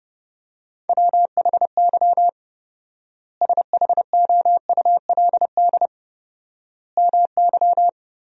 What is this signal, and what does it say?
Morse code, heard as a keyed tone, about 30 wpm, 700 Hz. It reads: WHY SHOULD MY
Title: 4x Word-Spacing - Sentences from Top 500 Words - 30wpm